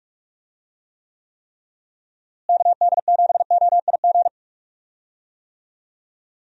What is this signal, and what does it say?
Morse code, heard as a keyed tone, about 44 wpm, 700 Hz. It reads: KD7OIG
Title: USA Call Signs - 6 Characters - Repeated 3x - 45wpm